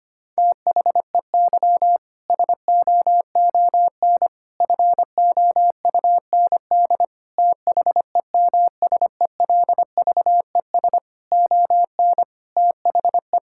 Morse code, heard as a keyed tone, about 25 wpm, 700 Hz. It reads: THEY SOON FOUND THEMSELVES ON THE
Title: Sentences from Top 1000 Words - Review Part 5 - 25wpm